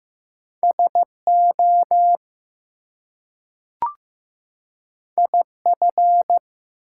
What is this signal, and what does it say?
Morse code, heard as a keyed tone, about 15 wpm, 700 Hz. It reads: SO IF
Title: Two Letter Words - 15wpm